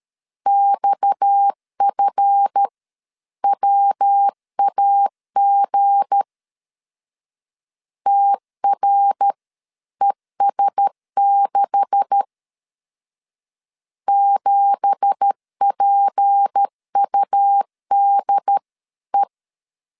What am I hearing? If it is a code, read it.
Morse code, heard as a keyed tone, about 13 wpm, spaced as for 11 wpm, 800 Hz. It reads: XFWAG TRES6 7PUDE
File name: CW / DET / 1 9 1500C S B 12wpm 3819